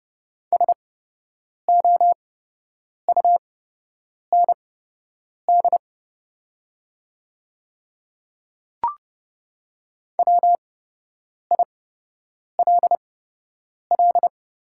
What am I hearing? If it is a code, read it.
Morse code, heard as a keyed tone, about 30 wpm, spaced as for 8 wpm, 700 Hz. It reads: SOUND WILL
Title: 8x Character Spacing - Top 100 Words - 30wpm